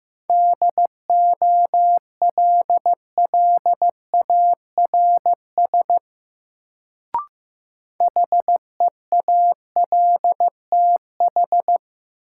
Morse code, heard as a keyed tone, about 15 wpm, 700 Hz. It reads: DOLLARS HEALTH
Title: Sets of 5 Words - New Word Drill - 15wpm